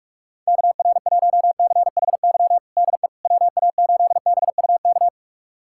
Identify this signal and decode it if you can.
Morse code, heard as a keyed tone, about 42 wpm, 690 Hz. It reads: KR1CHY DE WA8BUK
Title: USA Callsign DE Callsign - 6 Characters - 45wpm